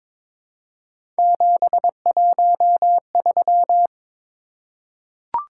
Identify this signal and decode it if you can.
Morse code, heard as a keyed tone, about 23 wpm, 700 Hz. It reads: 713
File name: practice / 22wpm / Three Letter-Number Combinations - 22wpm